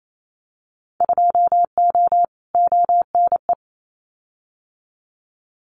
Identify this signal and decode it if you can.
Morse code, heard as a keyed tone, about 28 wpm, 700 Hz. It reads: 2O ONE
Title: Winter Field Day Exchanges Encore 28wpm